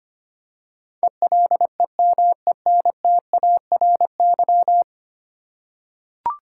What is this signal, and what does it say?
Morse code, heard as a keyed tone, about 25 wpm, 700 Hz. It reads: ELEMENTARY
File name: Ten Letter Words - 25wpm